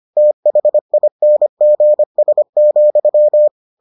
Morse code, heard as a keyed tone, about 25 wpm, 600 Hz. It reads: THINGS,